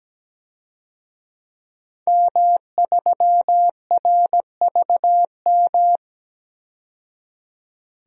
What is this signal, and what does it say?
Morse code, heard as a keyed tone, about 17 wpm, 700 Hz. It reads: M3RVM
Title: DXCC-100 Call Signs - 5 Characters - 17wpm